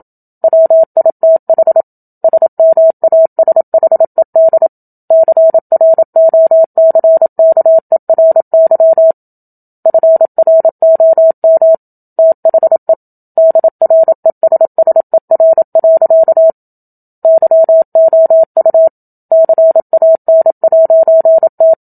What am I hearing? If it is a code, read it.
Morse code, heard as a keyed tone, about 27 wpm, 650 Hz. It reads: WITH SMASHED CROCKERY FROM THE DRESSER. YOU CAN'T
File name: Book Two / 01-Chapter 1 - 27 WPM